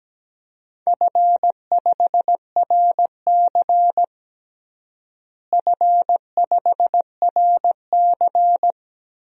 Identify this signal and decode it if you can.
Morse code, heard as a keyed tone, about 17 wpm, 700 Hz. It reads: F5RC F5RC
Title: DXCC-100 Call Signs - 4 Characters - Repeated 2x - 17wpm